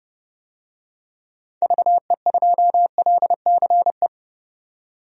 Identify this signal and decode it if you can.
Morse code, heard as a keyed tone, about 32 wpm, 700 Hz. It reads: VE2LCE
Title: DXCC-100 Call Signs - 6 Characters - 30wpm